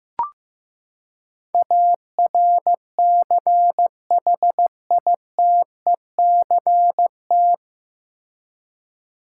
Morse code, heard as a keyed tone, about 15 wpm, 700 Hz. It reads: ARCHITECT
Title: K1USN SST QSO Practice 15wpm